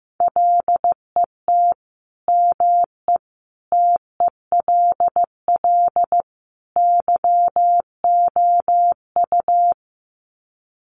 Morse code, heard as a keyed tone, about 15 wpm, 700 Hz. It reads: LET ME TELL YOU
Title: Sets of 4 Words 15wpm